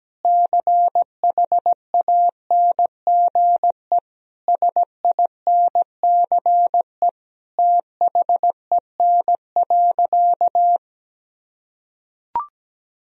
Morse code, heard as a keyed tone, about 17 wpm, 700 Hz. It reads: CHANGE SINCE THEN.
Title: Sentences from Top 1000 Words - Review Part 1 - 17wpm